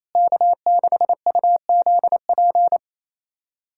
Morse code, heard as a keyed tone, about 28 wpm, 700 Hz. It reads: K6UZP
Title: USA Callsign DE Callsign - 5 Characters - 28wpm